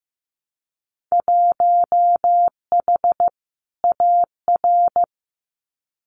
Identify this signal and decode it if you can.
Morse code, heard as a keyed tone, about 15 wpm, 700 Hz. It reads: 1H AR
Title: Winter Field Day Exchanges 15wpm